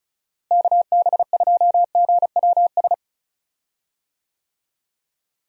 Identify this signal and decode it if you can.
Morse code, heard as a keyed tone, about 36 wpm, 700 Hz. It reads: KB2GWS
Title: USA Call Signs - 6 Characters - Repeated 2x - 35wpm